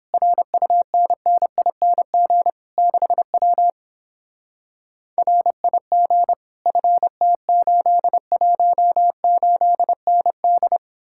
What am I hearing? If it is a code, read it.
Morse code, heard as a keyed tone, about 29 wpm, 700 Hz. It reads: RUNNING 6W RIG FT818ND